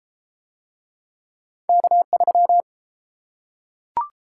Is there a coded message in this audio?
Yes: Morse code from 1.7 to 2.6 s, heard as a keyed tone, about 32 wpm, 700 Hz. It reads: K3